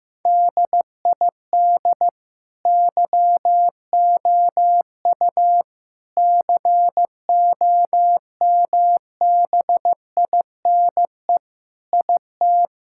Morse code, heard as a keyed tone, about 15 wpm, 700 Hz. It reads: DID YOU COMBINE IT
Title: Sentences from Top 1600 Words 15wpm